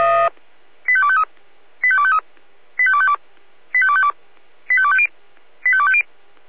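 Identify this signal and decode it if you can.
Five-tone selective call (ZVEI), heard as quick runs of five tones, tones of about 70 ms. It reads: 85253 85253 85252 85252 85250 85250
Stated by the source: Five-tone ZVEI selective call received by SDR.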